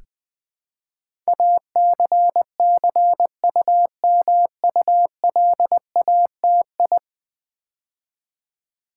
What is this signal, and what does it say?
Morse code, heard as a keyed tone, about 20 wpm, 700 Hz. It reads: ACCUMULATI
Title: Build A Word - 5 Letters At A Time - 12-Letter Words - 20wpm